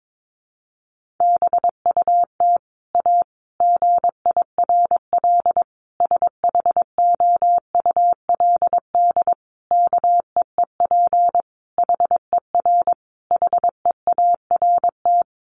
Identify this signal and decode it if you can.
Morse code, heard as a keyed tone, about 22 wpm, 700 Hz. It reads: BUT A GIRL SHOULD KEEP HER HEART
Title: Sentences from Top 1200 Words 22wpm